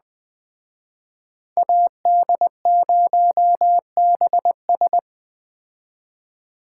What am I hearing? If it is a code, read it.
Morse code, heard as a keyed tone, about 21 wpm, 700 Hz. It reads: AD0BS